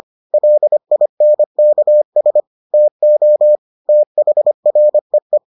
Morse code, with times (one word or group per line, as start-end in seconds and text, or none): 0.3-2.4 LINKS
2.7-3.6 TO
3.9-5.4 THREE